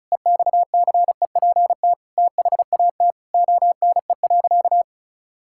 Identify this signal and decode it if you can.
Morse code, heard as a keyed tone, about 36 wpm, 700 Hz. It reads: EXCEPT THAT ONE.
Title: Sentences from Top 1600 Words 35wpm